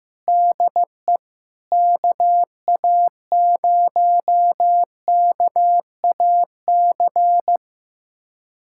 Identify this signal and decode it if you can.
Morse code, heard as a keyed tone, about 15 wpm, 700 Hz. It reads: DE KA0KAC
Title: USA Callsign DE Callsign - 6 Characters - 15wpm